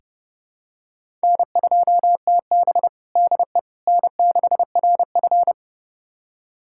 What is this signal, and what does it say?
Morse code, heard as a keyed tone, about 30 wpm, 700 Hz. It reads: N2TB DE N6RF
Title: USA Callsign DE Callsign - 4 Characters - 30wpm